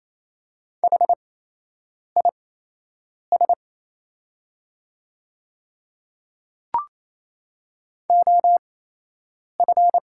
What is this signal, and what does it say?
Morse code, heard as a keyed tone, about 28 wpm, spaced as for 8 wpm, 700 Hz. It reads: HIS OF
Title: 8x Character Spacing - Top 200 Words - 28wpm